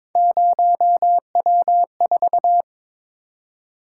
Morse code, heard as a keyed tone, about 22 wpm, 700 Hz. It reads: 0W4
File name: practice / 22wpm / Three Letter-Number Combinations - 22wpm